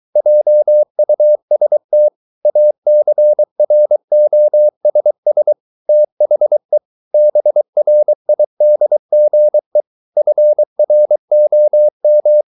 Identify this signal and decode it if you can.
Morse code, heard as a keyed tone, about 23 wpm, 600 Hz. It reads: JUST ACROSS THE BRIDGE FROM